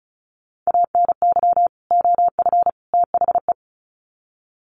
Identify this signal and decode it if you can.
Morse code, heard as a keyed tone, about 34 wpm, 700 Hz. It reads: ANY OF THE
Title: Sets of 3 Words 35wpm